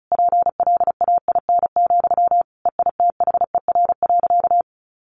Morse code, heard as a keyed tone, about 34 wpm, 700 Hz. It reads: PLAIN, EITHER.
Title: Sentences from Top 600 Words 35wpm